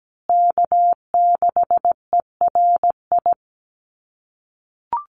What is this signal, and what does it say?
Morse code, heard as a keyed tone, about 17 wpm, 700 Hz. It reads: K6ERI
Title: USA Call Signs Repeated 3x 17wpm